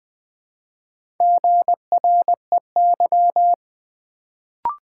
Morse code, heard as a keyed tone, about 20 wpm, 700 Hz. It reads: GREY
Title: Build A Word - 1 Letter At A Time - 4-Letter Words - 20wpm